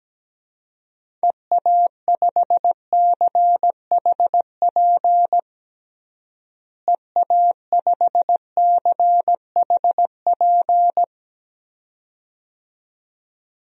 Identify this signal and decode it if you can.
Morse code, heard as a keyed tone, about 17 wpm, 700 Hz. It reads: EA5CHP EA5CHP
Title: DXCC-100 Call Signs - 6 Characters - Repeated 2x - 17wpm